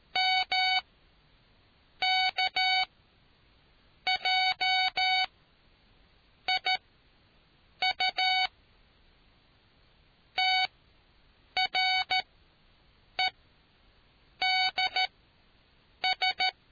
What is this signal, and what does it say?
Morse code, heard as a keyed tone, about 13 wpm, spaced as for 6 wpm, 750 Hz. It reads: MKJIU TREDS